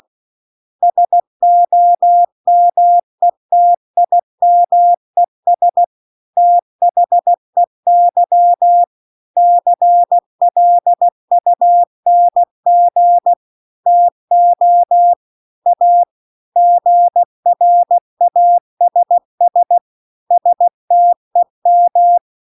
Morse code, heard as a keyed tone, about 16 wpm, 700 Hz. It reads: SOMETIMES THEY CLUNG TO A GRASS STEM
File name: 16WPM